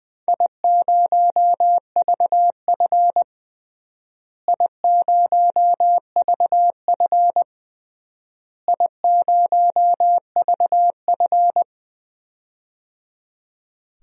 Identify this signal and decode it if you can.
Morse code, heard as a keyed tone, about 20 wpm, 700 Hz. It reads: I0VF I0VF I0VF
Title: DXCC-100 Call Signs - 4 Characters - Repeated 3x - 20wpm